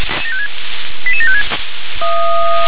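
Five-tone selective call (ZVEI), heard as quick runs of five tones, tones of about 80 ms. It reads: 88756
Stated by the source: ZVEI five-tone selective call received on an SDR.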